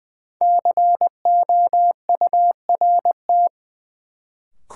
Morse code, heard as a keyed tone, about 20 wpm, 700 Hz. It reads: COURT
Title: Sentences from Top 1600 Words - New Word Drill Alt - 20wpm